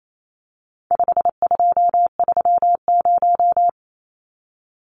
Morse code, heard as a keyed tone, about 28 wpm, 700 Hz. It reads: 5230
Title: Numbers - 4-Digit 28wpm